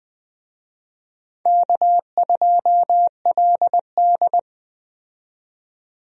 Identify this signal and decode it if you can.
Morse code, heard as a keyed tone, about 21 wpm, 700 Hz. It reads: K2LD
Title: USA Call Signs - 4 Characters - 20wpm